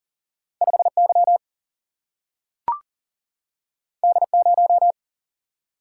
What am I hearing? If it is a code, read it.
Morse code, heard as a keyed tone, about 42 wpm, 700 Hz. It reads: 5 Y D 0
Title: Single-Letter Single-Number 40wpm